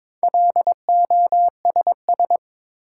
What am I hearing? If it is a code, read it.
Morse code, heard as a keyed tone, about 22 wpm, 700 Hz. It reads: LOSS
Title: Sentences from Top 1400 Words - New Word Drill Alt - 22wpm